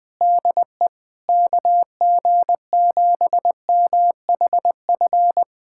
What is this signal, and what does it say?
Morse code, heard as a keyed tone, about 20 wpm, 700 Hz. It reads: DE KG7MHF
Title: USA Callsign DE Callsign - 6 Characters - 20wpm